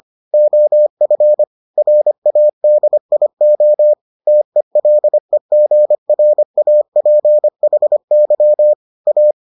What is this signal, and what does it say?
Morse code, heard as a keyed tone, about 25 wpm, 600 Hz. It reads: OF RADIO TELEGRAPHY A